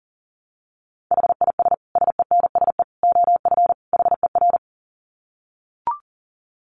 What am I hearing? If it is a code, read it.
Morse code, heard as a keyed tone, about 40 wpm, 700 Hz. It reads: HIS SENSE OF HER
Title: Dit Challenge 40wpm